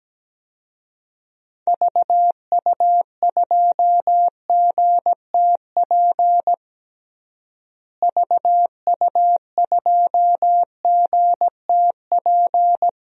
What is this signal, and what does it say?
Morse code, heard as a keyed tone, about 17 wpm, 700 Hz. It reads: VU2GTP VU2GTP
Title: DXCC-100 Call Signs - 6 Characters - Repeated 3x - 17wpm